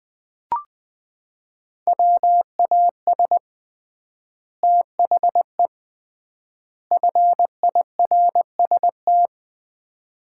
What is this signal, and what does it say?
Morse code, heard as a keyed tone, about 21 wpm, 700 Hz. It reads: WAS THE FIRST
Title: 3x Word-Spacing - Sets of 3 Words - 20wpm